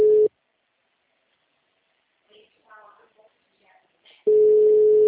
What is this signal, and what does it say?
Ringback tone (Europe, 425 Hz), with speech.